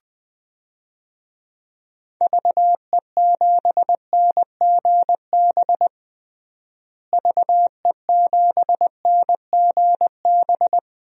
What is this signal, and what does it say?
Morse code, heard as a keyed tone, about 21 wpm, 700 Hz. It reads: VE7NGB VE7NGB
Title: DXCC-100 Call Signs - 6 Characters - Repeated 2x - 20wpm